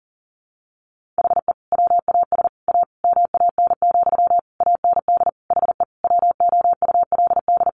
Morse code, heard as a keyed tone, about 40 wpm, 700 Hz. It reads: HE WAS A MAN, AND HE WOULD
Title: Sentences from Top 1500 Words 40wpm